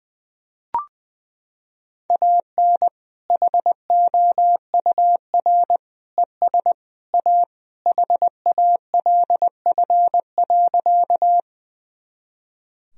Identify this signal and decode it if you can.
Morse code, heard as a keyed tone, about 20 wpm, 700 Hz. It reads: AN HOUR ES A HALF.
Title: Sentences from Top 400 Words and Abbreviations 20wpm